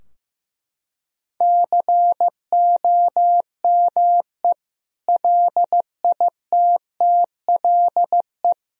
Morse code, heard as a keyed tone, about 15 wpm, 700 Hz. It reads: COME LITTLE
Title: Sentences from Top 1000 Words - Review Part 2 - 15wpm